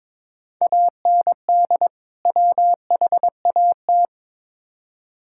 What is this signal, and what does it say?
Morse code, heard as a keyed tone, about 22 wpm, 700 Hz. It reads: AND WHAT
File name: Build A Phrase - 2-Word Phrases - 22wpm